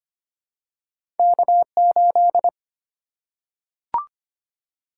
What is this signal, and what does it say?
Morse code, heard as a keyed tone, about 25 wpm, 700 Hz. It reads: K8
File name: Two Letter-Number Combinations - 25wpm